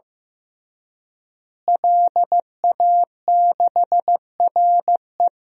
Morse code, heard as a keyed tone, about 15 wpm, 700 Hz. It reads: LA6RE
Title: DXCC-100 Call Signs - 5 Characters - Repeated 3x - 15wpm